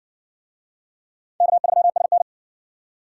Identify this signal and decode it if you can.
Morse code, heard as a keyed tone, about 59 wpm, 700 Hz. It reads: K3SN